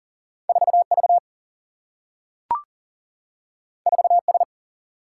Morse code, heard as a keyed tone, about 40 wpm, 700 Hz. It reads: 4 V 4 S
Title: Single-Letter Single-Number 40wpm